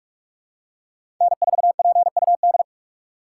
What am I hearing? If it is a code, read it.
Morse code, heard as a keyed tone, about 46 wpm, 700 Hz. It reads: N4PUD
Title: USA Call Signs - 5 Characters - 3x Character Spacing - 45wpm